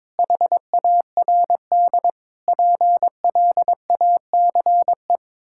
Morse code, heard as a keyed tone, about 22 wpm, 700 Hz. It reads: HARD PLACE
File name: practice / 22wpm / Sets of 8 Words - 22wpm